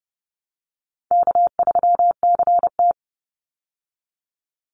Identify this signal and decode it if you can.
Morse code, heard as a keyed tone, about 30 wpm, 700 Hz. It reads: K3CT
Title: Call Signs Repeated 2x - 30wpm